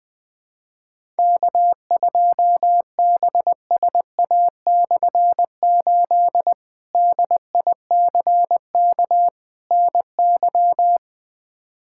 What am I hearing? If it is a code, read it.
Morse code, heard as a keyed tone, about 20 wpm, 700 Hz. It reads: K2BSA/8 DICK NY
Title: K1USN SST QSO Practice - 20wpm